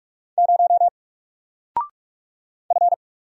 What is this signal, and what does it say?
Morse code, heard as a keyed tone, about 44 wpm, 700 Hz. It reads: 0 F
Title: Single Letter-Number Alt 45wpm